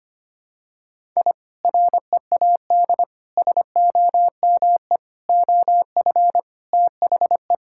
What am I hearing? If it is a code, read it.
Morse code, heard as a keyed tone, about 25 wpm, 700 Hz. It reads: I READ SOME OF THE
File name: practice / 25wpm / Sentences with 2-Word Phrases - 25wpm